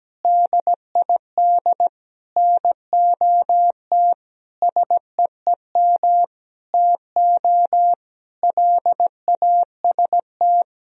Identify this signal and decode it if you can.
Morse code, heard as a keyed tone, about 17 wpm, 700 Hz. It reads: DID NOT SEEM TO LAST
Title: Sentences from Top 1000 Words - Review Part 1 - 17wpm